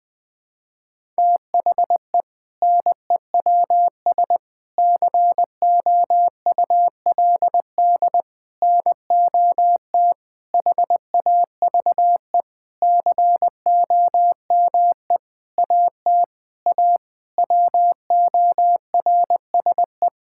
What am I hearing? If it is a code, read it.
Morse code, heard as a keyed tone, about 20 wpm, 700 Hz. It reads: THE NEWS COULD NOT HAVE COME AT A WORSE
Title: Sentences with 8-Word Phrases - 20wpm